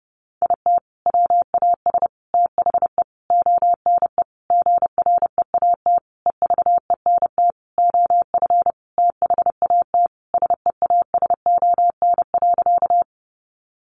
Morse code, heard as a keyed tone, about 30 wpm, 700 Hz. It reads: IT WAS THE ONE GREAT EVENT OF THAT SEASON.
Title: Sentences from Top 1400 Words 30wpm